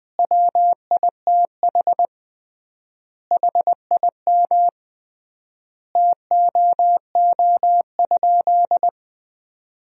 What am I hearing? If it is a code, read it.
Morse code, heard as a keyed tone, about 21 wpm, 700 Hz. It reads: WITH HIM TOO?
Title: 3x Word-Spacing - Sentences from Top 500 Words - 20wpm